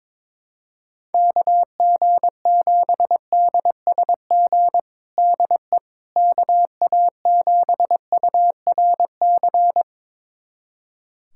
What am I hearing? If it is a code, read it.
Morse code, heard as a keyed tone, about 22 wpm, 700 Hz. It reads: KG7DSG DE KA7URC
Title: USA Callsign DE Callsign - 6 Characters - 22wpm